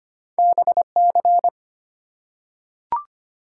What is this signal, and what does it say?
Morse code, heard as a keyed tone, about 25 wpm, 700 Hz. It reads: BC
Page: morse-code-ninja-practice-25wpm